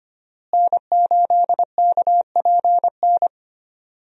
Morse code, heard as a keyed tone, about 25 wpm, 700 Hz. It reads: N8KPN
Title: USA Call Signs - 5 Characters - Repeated 3x - 25wpm